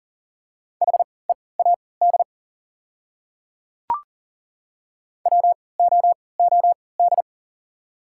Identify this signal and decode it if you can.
Morse code, heard as a keyed tone, about 40 wpm, spaced as for 23 wpm, 700 Hz. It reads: HEAD WOOD
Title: 3x Character Spacing - Top 300 Words - 40wpm